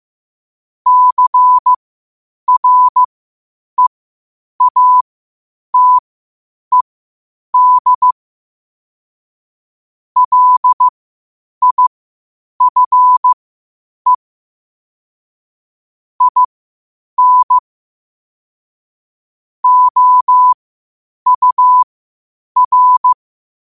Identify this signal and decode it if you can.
Morse code, heard as a keyed tone, about 15 wpm, spaced as for 8 wpm, 1000 Hz. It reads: CREATED LIFE IN OUR
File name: qotdfiles / 8wpm